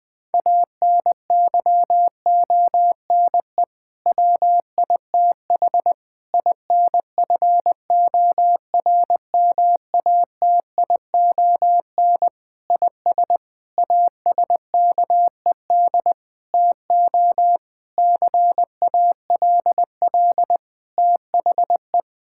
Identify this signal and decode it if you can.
Morse code, heard as a keyed tone, about 20 wpm, 700 Hz. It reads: ANYONE WITH INFORMATION IS ASKED TO CALL THE
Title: Sentences with 6-Word Phrases - 20wpm